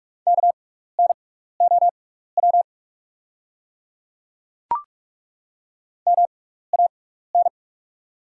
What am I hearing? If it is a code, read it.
Morse code, heard as a keyed tone, about 43 wpm, spaced as for 15 wpm, 690 Hz. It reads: KNOW MAN
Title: 6x Character Spacing - Top 300 Words - 45wpm